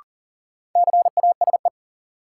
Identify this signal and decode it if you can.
Morse code, heard as a keyed tone, about 38 wpm, 700 Hz. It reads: CASE
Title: Sentences from Top 700 Words - New Word Drill - 40wpm